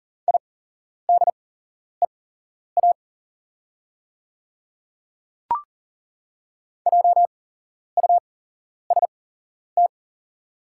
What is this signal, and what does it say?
Morse code, heard as a keyed tone, about 40 wpm, spaced as for 11 wpm, 700 Hz. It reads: IDEA JUST
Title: 8x Character Spacing - Top 500 Words - 40wpm